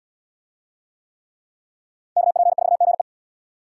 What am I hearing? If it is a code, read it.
Morse code, heard as a keyed tone, about 71 wpm, 700 Hz. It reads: KP4GE